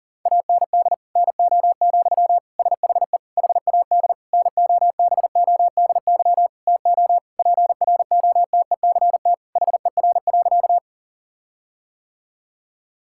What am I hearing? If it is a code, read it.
Morse code, heard as a keyed tone, about 37 wpm, 700 Hz. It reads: AND NO, SHE HAD NOBODY TO PROTECT HER.
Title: Sentences from Top 1600 Words 40wpm